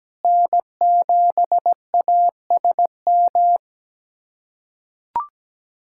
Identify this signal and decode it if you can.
Morse code, heard as a keyed tone, about 17 wpm, 700 Hz. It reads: N7ASM
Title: USA Callsign DE Callsign - 5 Characters - 17wpm